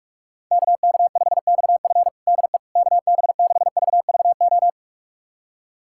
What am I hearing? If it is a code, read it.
Morse code, heard as a keyed tone, about 46 wpm, 700 Hz. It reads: KK5XF DE KB6VVO
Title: USA Callsign DE Callsign 45wpm